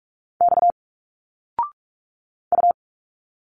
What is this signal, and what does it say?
Morse code, heard as a keyed tone, about 46 wpm, 700 Hz. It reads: X U
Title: Single Letters 45wpm